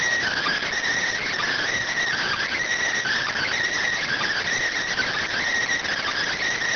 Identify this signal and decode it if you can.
Emergency Alert System (SAME) tones (header bursts), heard as harsh buzzing bursts. It is heard from 0.2 to 6.4 s.